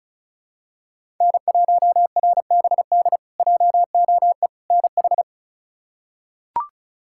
Morse code, heard as a keyed tone, about 36 wpm, 700 Hz. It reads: N1RBD JOE NH
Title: K1USN SST QSO Practice - 35wpm